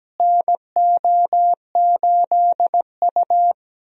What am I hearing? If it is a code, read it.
Morse code, heard as a keyed tone, about 17 wpm, 700 Hz. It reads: NO8U
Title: USA Call Signs - 4 Characters - Repeated 3x - 17wpm